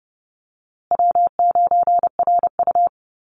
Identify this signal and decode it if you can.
Morse code, heard as a keyed tone, about 30 wpm, 700 Hz. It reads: W9RU
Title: USA Call Signs Repeated 2x 30wpm